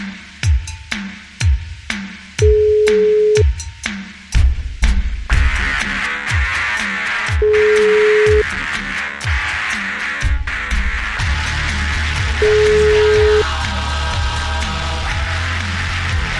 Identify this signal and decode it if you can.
Ringback tone (Europe, 425 Hz), with noise.